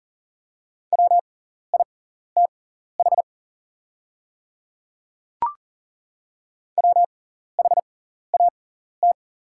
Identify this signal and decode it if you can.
Morse code, heard as a keyed tone, about 40 wpm, spaced as for 14 wpm, 700 Hz. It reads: WITH WHAT
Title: 6x Character Spacing - Top 200 Words - 40wpm